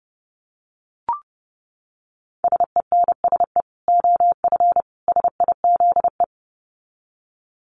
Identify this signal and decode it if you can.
Morse code, heard as a keyed tone, about 30 wpm, 700 Hz. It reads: SENSE OF SIZE
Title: Dit Challenge 30wpm